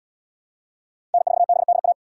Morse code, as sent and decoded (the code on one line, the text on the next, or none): -. ....- -... -.. .-
N4BDA